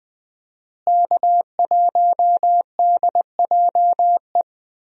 Morse code, heard as a keyed tone, about 20 wpm, 700 Hz. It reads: K1DJE
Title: USA Call Signs - 5 Characters - Repeated 3x - 20wpm